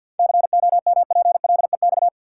Morse code, heard as a keyed tone, about 46 wpm, 700 Hz. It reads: COMPLEX
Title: Sentences from Top 900 Words - New Word Drill - 50wpm